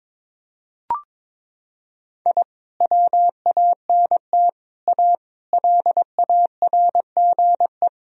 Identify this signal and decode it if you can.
Morse code, heard as a keyed tone, about 22 wpm, 700 Hz. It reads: I WANT A LARGE
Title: Sentences from Top 1000 Words - Review Part 1 - 22wpm